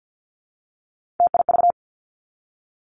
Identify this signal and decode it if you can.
Morse code, heard as a keyed tone, about 50 wpm, 700 Hz. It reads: TIV
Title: Three Challenging Letter-Number Combinations 50wpm